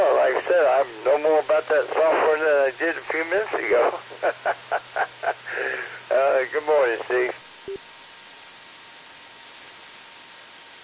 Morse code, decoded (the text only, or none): none